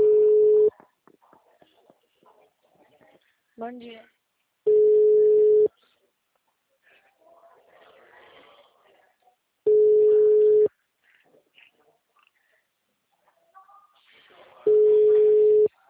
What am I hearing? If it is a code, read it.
Ringback tone (Europe, 425 Hz), with speech.